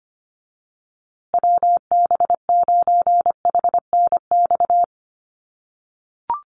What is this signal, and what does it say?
Morse code, heard as a keyed tone, about 25 wpm, 700 Hz. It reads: WB9HNX
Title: USA Call Signs Repeated 3x 25wpm